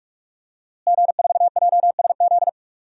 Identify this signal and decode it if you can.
Morse code, heard as a keyed tone, about 46 wpm, 700 Hz. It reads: G4JSZ
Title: DXCC-100 Call Signs - 5 Characters - Repeated 3x - 45wpm